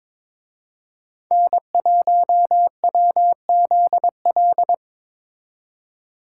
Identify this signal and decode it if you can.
Morse code, heard as a keyed tone, about 24 wpm, 700 Hz. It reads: N1WZL
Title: Speed Racing - USA 1x3 Call Signs 22wpm